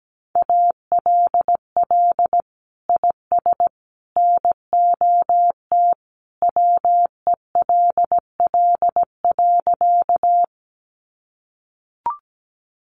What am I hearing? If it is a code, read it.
Morse code, heard as a keyed tone, about 17 wpm, 700 Hz. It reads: ALL IS NOT WELL.
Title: Sentences from Top 600 Words 17wpm